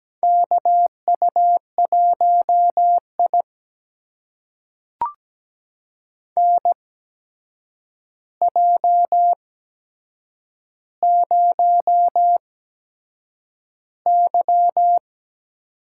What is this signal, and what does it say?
Morse code, heard as a keyed tone, about 17 wpm, spaced as for 7 wpm, 700 Hz. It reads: KU1I NJ0Y
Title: USA Call Signs - 4 Characters - 8x Character Spacing - 17wpm